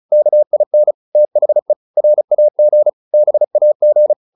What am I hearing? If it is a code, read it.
Morse code, heard as a keyed tone, about 34 wpm, 600 Hz. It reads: KIN THE RAG BAG